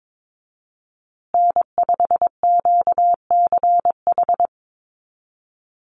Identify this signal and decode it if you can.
Morse code, heard as a keyed tone, about 22 wpm, 700 Hz. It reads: N5QCH
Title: USA Call Signs Repeated 3x 22wpm